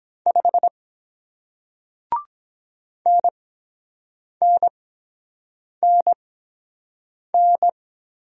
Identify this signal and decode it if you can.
Morse code, heard as a keyed tone, about 23 wpm, spaced as for 7 wpm, 700 Hz. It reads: 5 NNNN